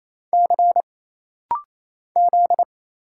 Morse code, heard as a keyed tone, about 27 wpm, 700 Hz. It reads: C Z
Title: Single Letter-Number Alt - Rapid-Fire 28wpm